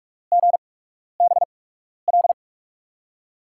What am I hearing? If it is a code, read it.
Morse code, heard as a keyed tone, about 47 wpm, spaced as for 12 wpm, 700 Hz. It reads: GBL
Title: USA Call Signs - 6 Characters - 8x Character Spacing - 45wpm